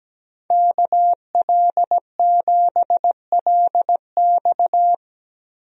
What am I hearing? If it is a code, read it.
Morse code, heard as a keyed tone, about 17 wpm, 700 Hz. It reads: KL7LX
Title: DXCC-100 Call Signs - 5 Characters - 17wpm